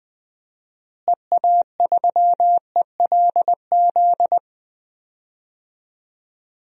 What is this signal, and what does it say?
Morse code, heard as a keyed tone, about 20 wpm, 700 Hz. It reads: EA3ELZ
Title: DXCC-100 Call Signs - 6 Characters - Repeated 2x - 20wpm